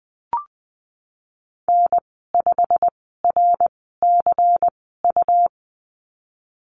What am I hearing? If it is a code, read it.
Morse code, heard as a keyed tone, about 20 wpm, spaced as for 15 wpm, 700 Hz. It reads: N5RCU
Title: 179 Lesson 29g - Slash - Call Signs 20wpm